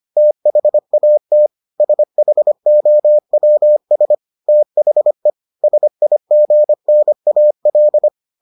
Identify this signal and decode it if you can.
Morse code, heard as a keyed tone, about 25 wpm, 600 Hz. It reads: THAT SHOWS THE SIGNAL